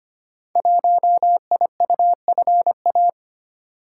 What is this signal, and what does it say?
Morse code, heard as a keyed tone, about 25 wpm, 700 Hz. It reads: 1IUFA
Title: Code Groups - 5 Letter:Numbers - 25wpm